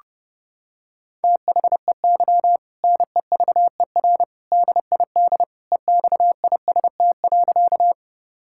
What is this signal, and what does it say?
Morse code, heard as a keyed tone, about 30 wpm, 700 Hz. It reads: THEY NEVER DID EXIST.
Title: Sentences from Top 1000 Words - Review Part 5 - 30wpm